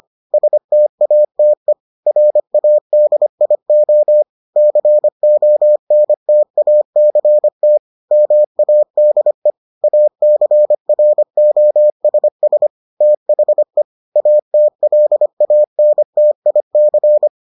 Morse code, heard as a keyed tone, about 25 wpm, 600 Hz. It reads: STATE RADIO CONTACT MADE ACROSS THE ATLANTIC